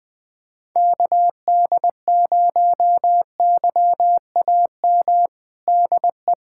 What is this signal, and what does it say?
Morse code, heard as a keyed tone, about 20 wpm, 700 Hz. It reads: KD0YAM DE
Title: USA Callsign DE Callsign - 6 Characters - 20wpm